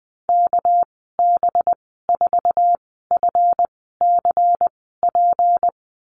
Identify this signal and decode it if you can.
Morse code, heard as a keyed tone, about 20 wpm, spaced as for 14 wpm, 700 Hz. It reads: KB4FCP
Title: 245 Lesson 39g - Letter z - Call Signs 20wpm